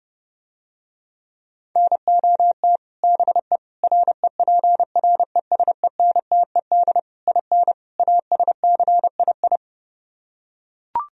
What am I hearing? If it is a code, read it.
Morse code, heard as a keyed tone, about 30 wpm, 700 Hz. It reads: NOT BE REPRESENTED IN ASCII